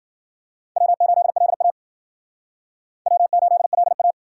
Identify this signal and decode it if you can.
Morse code, heard as a keyed tone, about 52 wpm, 700 Hz. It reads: W8LA W8LA